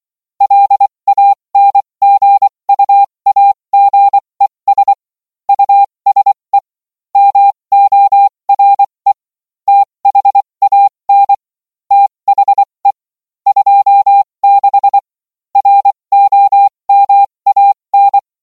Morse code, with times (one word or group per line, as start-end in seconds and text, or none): 0.4-4.9 LANGUAGES
5.5-6.6 USE
7.2-9.1 MORE
9.7-11.3 THAN
11.9-12.9 THE
13.5-15.0 26
15.6-18.2 ROMAN